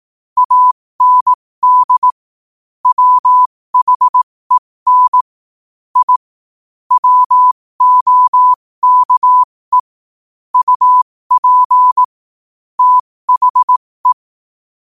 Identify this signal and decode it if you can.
Morse code, heard as a keyed tone, about 18 wpm, spaced as for 15 wpm, 1000 Hz. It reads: AND WHEN I WOKE UP THE